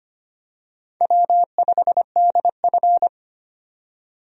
Morse code, heard as a keyed tone, about 25 wpm, 700 Hz. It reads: W5DF
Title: USA Call Signs - 4 Characters - 25wpm